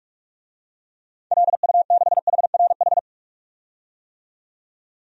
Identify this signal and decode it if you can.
Morse code, heard as a keyed tone, about 43 wpm, 700 Hz. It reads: LU6HRH
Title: DXCC-100 Call Signs - 6 Characters - 45wpm